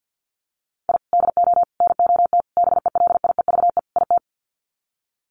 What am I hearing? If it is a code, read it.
Morse code, heard as a keyed tone, about 50 wpm, 700 Hz. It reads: I DO NOT BELIEVE IT
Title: Sets of 5 Words 50wpm